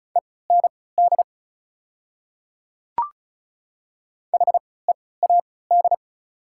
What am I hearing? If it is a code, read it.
Morse code, heard as a keyed tone, about 36 wpm, spaced as for 20 wpm, 690 Hz. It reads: END HEAD
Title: 3x Character Spacing - Top 500 Words - 35wpm